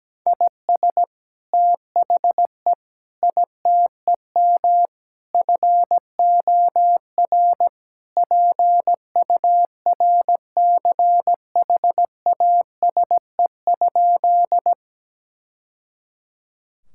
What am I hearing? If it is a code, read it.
Morse code, heard as a keyed tone, about 17 wpm, 700 Hz. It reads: IS THE ITEM FOR PURCHASE?
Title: Sentences from Top 1000 Words - Review Part 6 - 17wpm